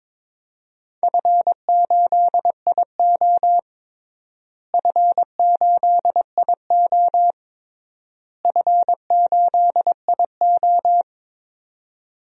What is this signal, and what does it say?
Morse code, heard as a keyed tone, about 21 wpm, 700 Hz. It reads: F8IO F8IO F8IO
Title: DXCC-100 Call Signs - 4 Characters - Repeated 3x - 22wpm